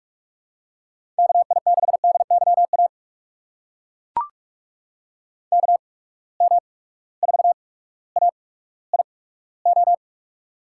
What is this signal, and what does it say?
Morse code, heard as a keyed tone, about 43 wpm, spaced as for 19 wpm, 700 Hz. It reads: KI6DYA KM4AIO
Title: USA Call Signs - 6 Characters - 8x Character Spacing - 45wpm